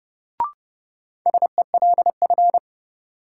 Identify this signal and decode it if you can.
Morse code, heard as a keyed tone, about 31 wpm, 700 Hz. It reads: SELF
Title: Sentences from Top 1000 Words - Review Part 1 - New Word Drill - 30wpm